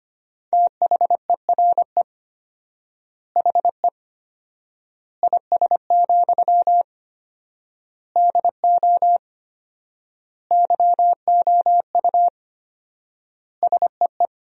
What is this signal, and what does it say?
Morse code, heard as a keyed tone, about 25 wpm, 700 Hz. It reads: THERE HE IS, DO YOU SEE
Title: 4x Word-Spacing - Sentences from Top 100 Words - 25wpm